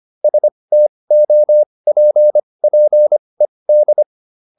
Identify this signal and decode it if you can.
Morse code, heard as a keyed tone, about 25 wpm, spaced as for 20 wpm, 600 Hz. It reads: STOPPED